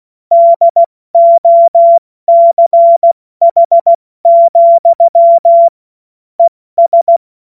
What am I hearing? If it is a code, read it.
Morse code, heard as a keyed tone, about 16 wpm, 680 Hz. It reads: DOCH, ES
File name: Morsekurs / Witze / Witz 13. April 2021, 70 F80